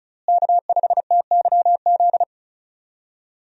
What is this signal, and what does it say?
Morse code, heard as a keyed tone, about 36 wpm, 700 Hz. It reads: K5TYZ
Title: USA Call Signs - 5 Characters - Repeated 3x - 35wpm